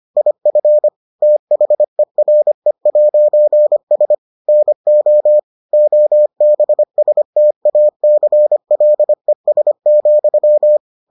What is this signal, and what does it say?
Morse code, heard as a keyed tone, about 25 wpm, 600 Hz. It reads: IF THERE'S NO OBSTACLES,